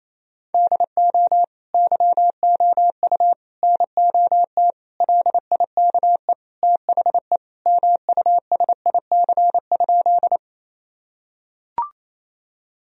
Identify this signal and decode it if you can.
Morse code, heard as a keyed tone, about 28 wpm, 700 Hz. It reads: DO YOU NOT LIKE THE MUSIC?
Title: Sentences from Top 1000 Words - Review Part 1 - 28wpm